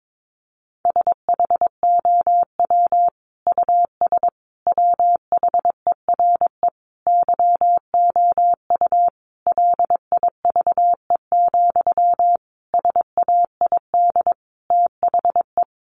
Morse code, heard as a keyed tone, about 22 wpm, 700 Hz. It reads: SHOW US WHERE YOU LIVE, SAID THE
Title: Sentences from Top 200 Words 22wpm